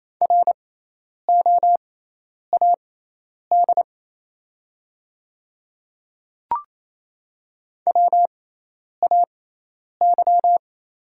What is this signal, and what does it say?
Morse code, heard as a keyed tone, about 28 wpm, spaced as for 10 wpm, 700 Hz. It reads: ROAD WAY
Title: 6x Character Spacing - Top 500 Words - 28wpm